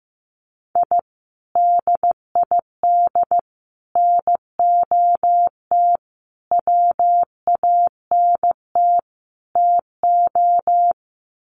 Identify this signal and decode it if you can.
Morse code, heard as a keyed tone, about 15 wpm, 700 Hz. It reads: I DID NOT WANT TO
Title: Sets of 5 Words 15wpm